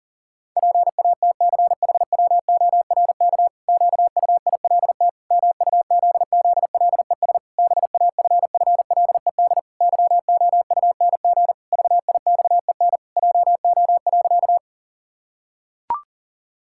Morse code, heard as a keyed tone, about 40 wpm, 700 Hz. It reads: PATCHWORK QUILT MUZZLES BAFFLED YOUNG VIXEN JO.
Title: Pangram Challenge v2 - 40wpm